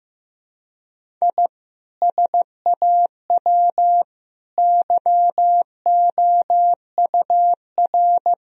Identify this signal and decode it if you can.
Morse code, heard as a keyed tone, about 15 wpm, 700 Hz. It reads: I SAW YOUR
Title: Sentences from Top 1000 Words - Review Part 1 - 15wpm